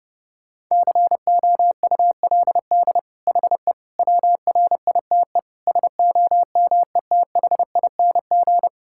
Morse code, heard as a keyed tone, about 30 wpm, 700 Hz. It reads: COULD HE WRITE SOMETHING
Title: Sentences from Top 1000 Words - Review Part 6 - 30wpm